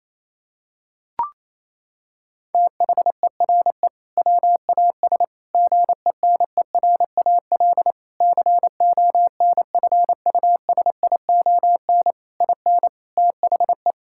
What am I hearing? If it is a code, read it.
Morse code, heard as a keyed tone, about 28 wpm, 700 Hz. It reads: THERE WAS GENERAL CONFUSION IN THE
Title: Sentences from Top 1600 Words 28wpm